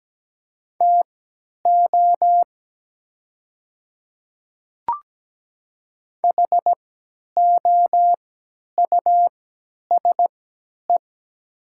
Morse code, heard as a keyed tone, about 17 wpm, spaced as for 10 wpm, 700 Hz. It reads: TO HOUSE
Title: 3x Character Spacing - Top 300 Words - 17wpm